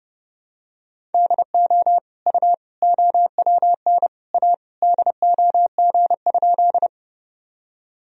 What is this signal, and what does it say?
Morse code, heard as a keyed tone, about 30 wpm, 700 Hz. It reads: DO U OWN A DOG?
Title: Sentences from Top 400 Words and Abbreviations 30wpm